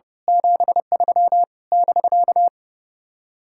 Morse code, heard as a keyed tone, about 29 wpm, 700 Hz. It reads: 73 <BK>
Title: Speed-Racing Common QSO Phrases - 30wpm